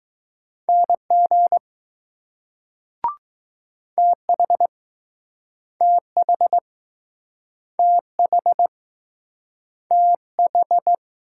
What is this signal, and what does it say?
Morse code, heard as a keyed tone, about 20 wpm, 700 Hz. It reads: NG TH TH TH TH